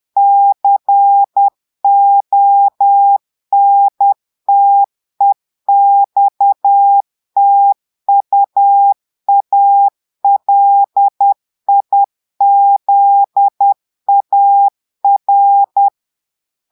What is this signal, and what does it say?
Morse code, heard as a keyed tone, about 10 wpm, 800 Hz. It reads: CONTEXTUALIZAR
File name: Portugais - texte en clair 1 - vitesse - 10 mots minute